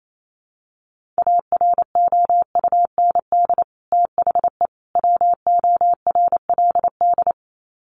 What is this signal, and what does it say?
Morse code, heard as a keyed tone, about 28 wpm, 700 Hz. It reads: AROUND THE WORLD
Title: Sets of 3 Words 28wpm